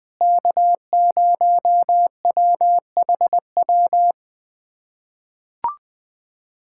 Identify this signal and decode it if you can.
Morse code, heard as a keyed tone, about 20 wpm, 700 Hz. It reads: K0WHW
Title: USA Callsign DE Callsign - 5 Characters - 20wpm